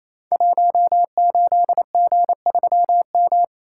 Morse code, heard as a keyed tone, about 27 wpm, 700 Hz. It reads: 18G3M
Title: Code Groups - 5 Letter:Numbers - 28wpm